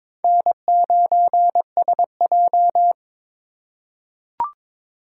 Morse code, heard as a keyed tone, about 22 wpm, 700 Hz. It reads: N9SJ
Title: USA Call Signs - 3x Character Spacing - 22wpm